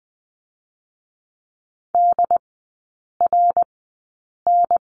Morse code, heard as a keyed tone, about 20 wpm, 700 Hz. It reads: D R N
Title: 100 Lesson 17e - Letter u - Three Characters 20wpm